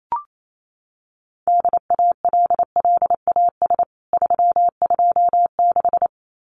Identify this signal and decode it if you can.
Morse code, heard as a keyed tone, about 28 wpm, 700 Hz. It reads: DALLAS 326
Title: CWT Exchange 28wpm